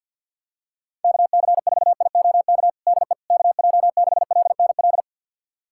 Morse code, heard as a keyed tone, about 48 wpm, 700 Hz. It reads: KC4IOK DE KJ6LNL
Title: USA Callsign DE Callsign - 6 Characters - 50wpm